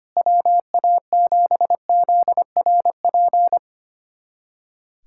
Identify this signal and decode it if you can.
Morse code, heard as a keyed tone, about 25 wpm, 700 Hz. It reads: WA7ZRP
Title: USA Callsign DE Callsign - 6 Characters - 25wpm